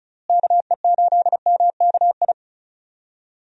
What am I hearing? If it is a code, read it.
Morse code, heard as a keyed tone, about 34 wpm, 700 Hz. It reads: KE8MKI
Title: USA Call Signs - 6 Characters - Repeated 3x - 35wpm